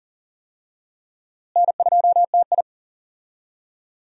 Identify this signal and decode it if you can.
Morse code, heard as a keyed tone, about 40 wpm, 700 Hz. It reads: N2TI
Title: USA Call Signs - 4 Characters - Repeated 2x - 40wpm